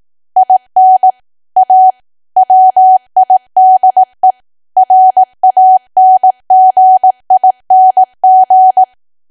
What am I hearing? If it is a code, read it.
Morse code, heard as a keyed tone, about 18 wpm, 750 Hz. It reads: IN A WIDE RANGING
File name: cwa / W1AW-Level2 / 18WPM